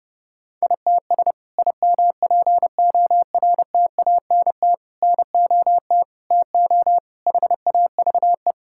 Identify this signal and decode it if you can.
Morse code, heard as a keyed tone, about 30 wpm, 700 Hz. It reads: ITS IMPORTANT NOT TO HAVE
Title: Sentences from Top 1000 Words - Review Part 5 - 30wpm